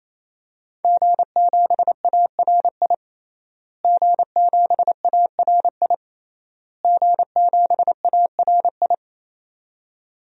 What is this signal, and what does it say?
Morse code, heard as a keyed tone, about 28 wpm, 700 Hz. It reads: G7ARI G7ARI G7ARI
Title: DXCC-100 Call Signs - 5 Characters - Repeated 3x - 28wpm